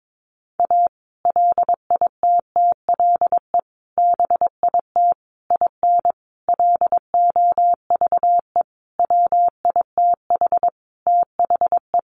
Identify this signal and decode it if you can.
Morse code, heard as a keyed tone, about 22 wpm, 700 Hz. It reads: A LITTLE BIT IN LOVE WITH THE
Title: Sentences from Top 900 Words 22wpm